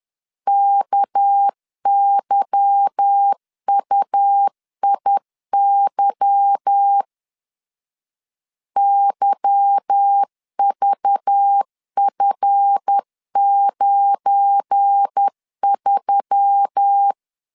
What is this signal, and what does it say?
Morse code, heard as a keyed tone, about 10 wpm, 800 Hz. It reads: KYUIY YVF93
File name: LEZIONI / 8 YJ45 / 8 Y 10wpm 2615